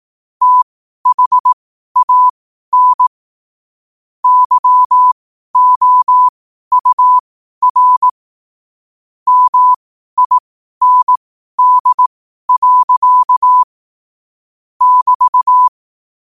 Morse code, heard as a keyed tone, about 18 wpm, spaced as for 12 wpm, 1000 Hz. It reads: THAN YOUR MIND. <BT>
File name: qotdfiles / 12wpm